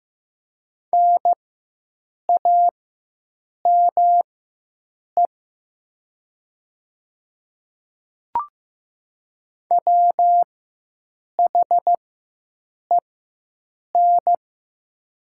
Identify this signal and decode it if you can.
Morse code, heard as a keyed tone, about 15 wpm, spaced as for 7 wpm, 700 Hz. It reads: NAME WHEN
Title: 4x Character Spacing - Top 300 Words - 15wpm